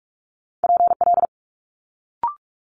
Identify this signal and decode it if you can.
Morse code, heard as a keyed tone, about 46 wpm, 700 Hz. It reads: PL
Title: Multi-letter Phonemes Quick Fire 45wpm